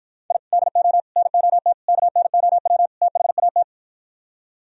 Morse code, heard as a keyed tone, about 58 wpm, 700 Hz. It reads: I DO NOT KNOW THAT